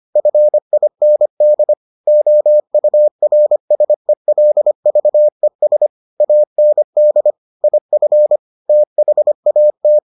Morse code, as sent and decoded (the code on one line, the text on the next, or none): ..-. .. -. -.. / --- ..- .-. ... .-.. ...- . ... / .- -. -.. / .. ..-. / - .... .- -
FIND OURSELVES AND IF THAT